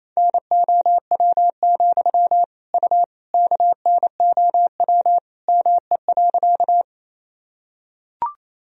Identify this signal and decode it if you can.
Morse code, heard as a keyed tone, about 28 wpm, 700 Hz. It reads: NOW, U KNOW ME.
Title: Sentences from Top 200 Words and Abbreviations 28wpm